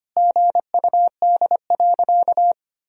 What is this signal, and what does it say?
Morse code, heard as a keyed tone, about 25 wpm, 700 Hz. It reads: GUD.
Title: Sentences from Top 500 Words and Abbreviations 25wpm